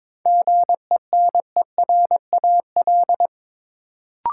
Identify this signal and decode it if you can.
Morse code, heard as a keyed tone, about 22 wpm, 700 Hz. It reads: GENERAL
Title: Sentences from Top 600 Words - New Word Drill - 22wpm